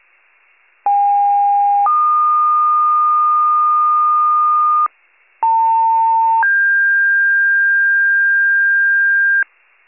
Two-tone page, heard as a one-second tone, then a three-second tone, about 810 then 1220 Hz, about 900 then 1600 Hz, with noise. It is heard from 0.9 to 9.4 s.